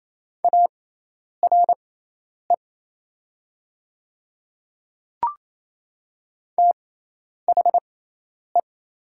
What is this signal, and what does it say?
Morse code, heard as a keyed tone, about 28 wpm, spaced as for 10 wpm, 700 Hz. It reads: ARE THE
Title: 6x Character Spacing - Top 500 Words - 28wpm